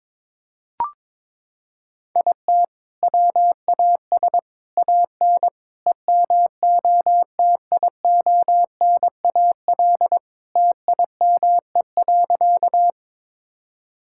Morse code, heard as a keyed tone, about 22 wpm, 700 Hz. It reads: IT WAS AN EMOTIONAL TIME.
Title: Sentences from Top 1000 Words - Review Part 5 - 22wpm